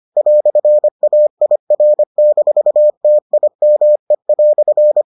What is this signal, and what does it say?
Morse code, heard as a keyed tone, about 25 wpm, 600 Hz. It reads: "AIR-TIME"